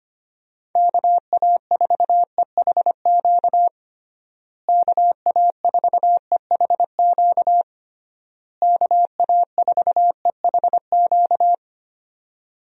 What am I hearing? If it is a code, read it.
Morse code, heard as a keyed tone, about 25 wpm, 700 Hz. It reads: KA4EHQ KA4EHQ KA4EHQ
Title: USA Call Signs - 6 Characters - Repeated 3x - 25wpm